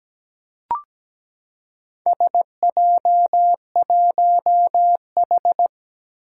Morse code, heard as a keyed tone, about 17 wpm, 700 Hz. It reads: SJ1H
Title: Four Challenging Letter-Number Combinations 17wpm